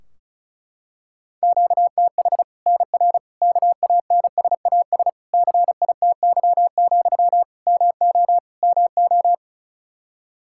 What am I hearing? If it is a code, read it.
Morse code, heard as a keyed tone, about 34 wpm, 700 Hz. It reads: QTH NR KANSAS CITY, MO MO
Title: Practice QSO - Exchange 1 - Beginning - 35wpm